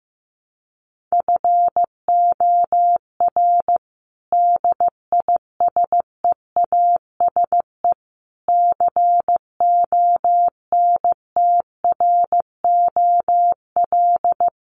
Morse code, heard as a keyed tone, about 15 wpm, 700 Hz. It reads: FOR DISEASE CONTROL
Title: Sets of 5 Words Encore2 15wpm